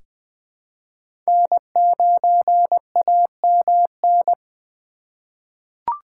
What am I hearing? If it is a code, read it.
Morse code, heard as a keyed tone, about 20 wpm, 700 Hz. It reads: N9AMN
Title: USA Call Signs - 5 Characters - 2x Character Spacing - 20wpm